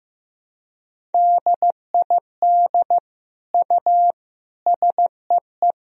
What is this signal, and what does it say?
Morse code, heard as a keyed tone, about 15 wpm, 700 Hz. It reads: DID U SEE
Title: Sentences from Top 100 Words and Abbreviations 15wpm